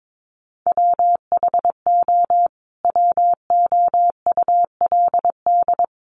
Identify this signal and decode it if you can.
Morse code, heard as a keyed tone, about 22 wpm, 700 Hz. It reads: WHO WOULD
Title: Sentences from Top 1500 Words 22wpm